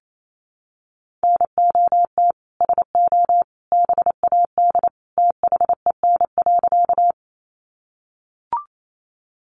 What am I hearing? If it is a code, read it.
Morse code, heard as a keyed tone, about 28 wpm, 700 Hz. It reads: NOT SO BAD THEN.
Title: Sentences from Top 700 Words 28wpm